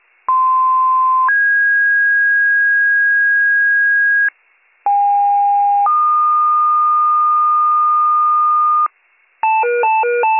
Two-tone page, heard as a one-second tone, then a three-second tone, about 1040 then 1650 Hz, about 810 then 1220 Hz, with noise. It is heard from 0.3 to 8.9 s.